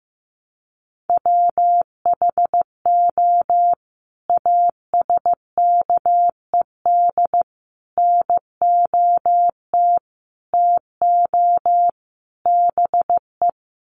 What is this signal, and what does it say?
Morse code, heard as a keyed tone, about 15 wpm, 700 Hz. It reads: WHO ASKED NOT TO BE
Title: Sets of 5 Words Encore2 15wpm